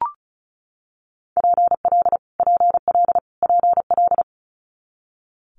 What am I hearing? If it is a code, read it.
Morse code, heard as a keyed tone, about 36 wpm, 700 Hz. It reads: PL PL PL
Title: Multi-letter Phonemes Repeated 3x 35wpm